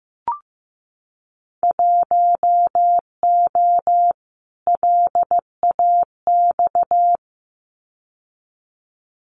Morse code, heard as a keyed tone, about 15 wpm, 700 Hz. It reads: 1O LAX
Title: Winter Field Day Exchanges 15wpm